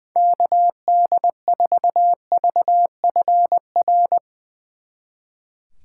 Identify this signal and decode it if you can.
Morse code, heard as a keyed tone, about 22 wpm, 700 Hz. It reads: KD4VFR
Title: Speed Racing - USA 2x3 Call Signs 20wpm